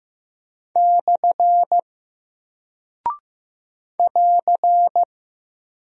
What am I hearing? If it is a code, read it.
Morse code, heard as a keyed tone, about 15 wpm, 700 Hz. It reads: / <AR>
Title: Single Character (Letter:Number:Punctuation:Prosign) 15wpm